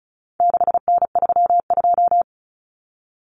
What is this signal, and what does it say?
Morse code, heard as a keyed tone, about 34 wpm, 700 Hz. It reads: 6N32
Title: Common Cut Numbers - 4-Digit 35wpm